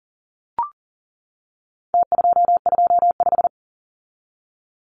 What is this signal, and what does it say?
Morse code, heard as a keyed tone, about 40 wpm, 700 Hz. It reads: T225
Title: Numbers - Contest Style - 4-Digit 40wpm